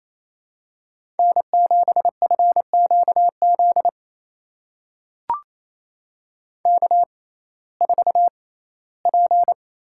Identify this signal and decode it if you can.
Morse code, heard as a keyed tone, about 28 wpm, spaced as for 17 wpm, 700 Hz. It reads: N7FQZ K4P
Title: USA Call Signs - 5 Characters - 6x Character Spacing - 28wpm